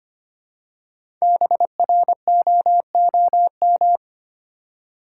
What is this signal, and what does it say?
Morse code, heard as a keyed tone, about 25 wpm, 690 Hz. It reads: BROOM
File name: practice / 25wpm / Easy Words with Hints - 25wpm